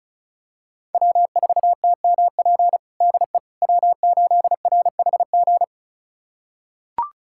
Morse code, heard as a keyed tone, about 34 wpm, 700 Hz. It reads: W4TMP DE W8RHG
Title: USA Callsign DE Callsign - 5 Characters - 35wpm